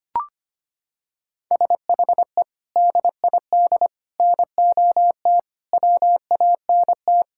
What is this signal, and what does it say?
Morse code, heard as a keyed tone, about 25 wpm, 700 Hz. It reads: SHE DID NOT WANT
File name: practice / 25wpm / Sentences with 4-Word Phrases - 25wpm